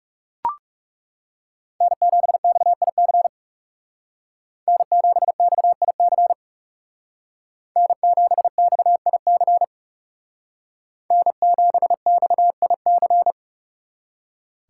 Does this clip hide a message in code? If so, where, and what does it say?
Yes: Morse code from 1.8 to 13.3 s, heard as a keyed tone, about 39 wpm, 700 Hz. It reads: N7XIC N7XIC N7XIC N7XIC